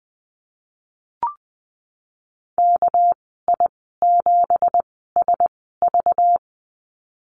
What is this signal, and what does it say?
Morse code, heard as a keyed tone, about 19 wpm, spaced as for 14 wpm, 700 Hz. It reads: KI7SV
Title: 200 Lesson 32g - Letter k - Call Signs 20wpm